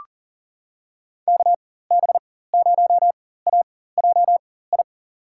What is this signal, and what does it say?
Morse code, heard as a keyed tone, about 40 wpm, spaced as for 19 wpm, 700 Hz. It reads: KB0AJI
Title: USA Call Signs - 4x Character Spacing - 40wpm